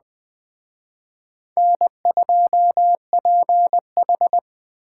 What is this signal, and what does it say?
Morse code, heard as a keyed tone, about 20 wpm, 700 Hz. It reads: N2PH